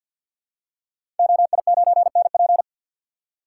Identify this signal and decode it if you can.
Morse code, heard as a keyed tone, about 52 wpm, 700 Hz. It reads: OI9NP
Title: Code Groups - 5 Letter:Numbers - 50wpm